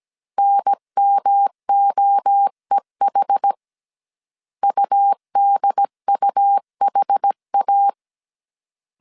Morse code, heard as a keyed tone, about 17 wpm, 800 Hz. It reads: NMOEH UDUHA
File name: CW / DET / 1 3 1500C S C 16wpm 2148